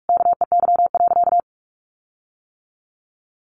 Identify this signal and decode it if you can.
Morse code, heard as a keyed tone, about 44 wpm, 700 Hz. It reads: KEY.
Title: Sentences from Top 1300 Words 45wpm